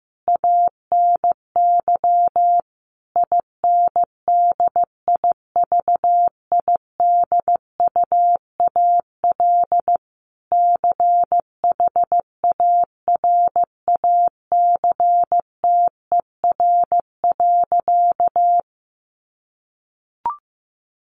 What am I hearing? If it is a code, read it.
Morse code, heard as a keyed tone, about 15 wpm, 700 Hz. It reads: ANY INDIVIDUAL CHARACTER.
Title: Sentences from Top 1500 Words 15wpm